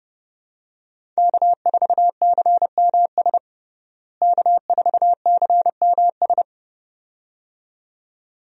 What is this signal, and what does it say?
Morse code, heard as a keyed tone, about 32 wpm, 700 Hz. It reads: K4CMS K4CMS
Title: USA Call Signs - 5 Characters - Repeated 2x - 30wpm